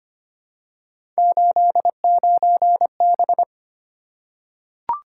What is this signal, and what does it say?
Morse code, heard as a keyed tone, about 25 wpm, 700 Hz. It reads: 89B
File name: Three Letter-Number Combinations - 25wpm